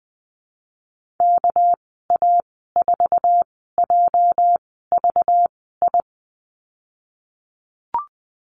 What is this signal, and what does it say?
Morse code, heard as a keyed tone, about 20 wpm, spaced as for 15 wpm, 700 Hz. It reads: KA4JVI
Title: 207 Lesson 33g - Letter j - Call Signs 20wpm